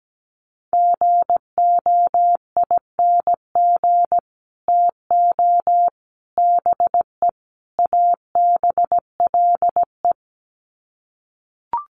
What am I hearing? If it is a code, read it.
Morse code, heard as a keyed tone, about 17 wpm, 700 Hz. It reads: GOING TO BE ABLE
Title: Sets of 4 Words 17wpm